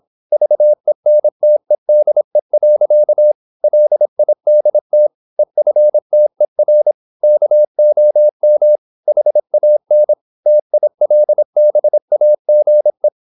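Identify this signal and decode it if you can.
Morse code, heard as a keyed tone, about 26 wpm, 600 Hz. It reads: VENTEDE. LIDT EFTER KOM HAN TILBAGE
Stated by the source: High-speed (QRQ) Morse at 26 wpm.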